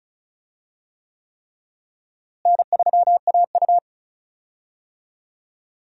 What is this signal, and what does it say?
Morse code, heard as a keyed tone, about 36 wpm, 700 Hz. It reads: N3AU
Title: USA Call Signs - 4 Characters - Repeated 3x - 35wpm